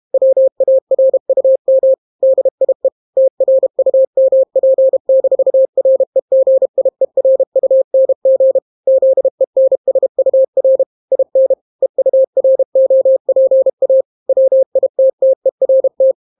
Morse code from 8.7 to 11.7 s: ZENSUR IN